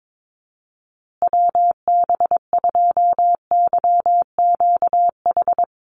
Morse code, heard as a keyed tone, about 22 wpm, 690 Hz. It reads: WB2YQH
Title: Call Signs Repeated 2x - 22wpm